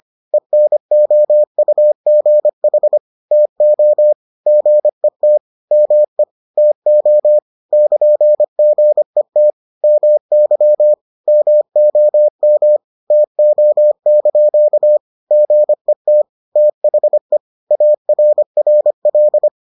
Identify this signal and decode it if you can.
Morse code, heard as a keyed tone, about 25 wpm, 600 Hz. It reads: ENOUGH TO GET ME TO <KN>GET MY MOM TO) GET THE ARRL